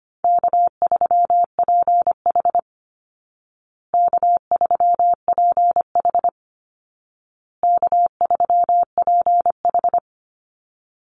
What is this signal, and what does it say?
Morse code, heard as a keyed tone, about 25 wpm, 700 Hz. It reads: K3PH K3PH K3PH
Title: Easy Call Signs Repeated 3x - 25wpm